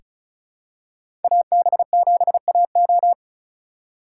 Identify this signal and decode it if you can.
Morse code, heard as a keyed tone, about 36 wpm, 690 Hz. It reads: AB7AO
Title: USA Call Signs - 3x Character Spacing - 35wpm